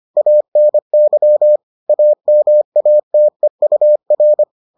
Morse code, heard as a keyed tone, about 25 wpm, 600 Hz. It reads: ANY AMATEUR